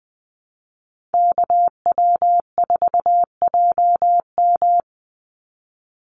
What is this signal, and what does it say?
Morse code, heard as a keyed tone, about 20 wpm, 700 Hz. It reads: KW4JM
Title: NA SOTA Chaser Callsigns 20wpm